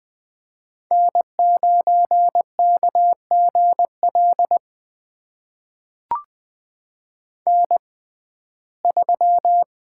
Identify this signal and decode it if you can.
Morse code, heard as a keyed tone, about 20 wpm, spaced as for 14 wpm, 700 Hz. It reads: N9KGL N3
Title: USA Call Signs - 5 Characters - 6x Character Spacing - 20wpm